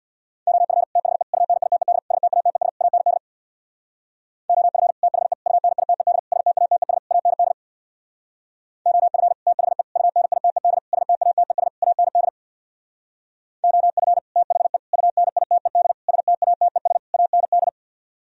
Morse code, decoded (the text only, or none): OF THE UNITED STATES AND OF THE UNITED STATES AND OF THE UNITED STATES AND OF THE UNITED STATES AND